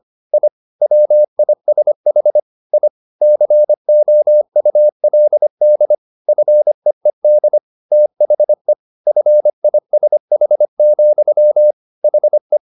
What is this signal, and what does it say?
Morse code, heard as a keyed tone, about 25 wpm, 600 Hz. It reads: I WISH I COULD FEED THE FISH, HE
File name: the old man and the sea part 1 25wpm